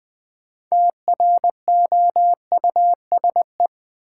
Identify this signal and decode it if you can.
Morse code, heard as a keyed tone, about 20 wpm, 700 Hz. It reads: TROUSE
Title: Build A Word - 3 Letters At A Time - 8-Letter Words - 20wpm